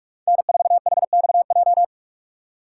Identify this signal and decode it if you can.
Morse code, heard as a keyed tone, about 42 wpm, 700 Hz. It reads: N4HXJ
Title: USA Call Signs - 5 Characters - 6x Character Spacing - 45wpm